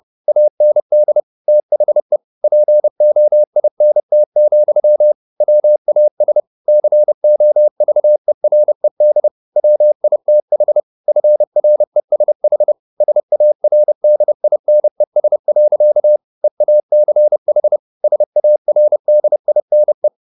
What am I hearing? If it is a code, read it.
Morse code, heard as a keyed tone, about 30 wpm, 600 Hz. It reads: AND THE POINT, WAS COVERED WITH FRESH SARDINES. EACH SARDINE
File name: the old man and the sea / the old man and the sea 1 30wpm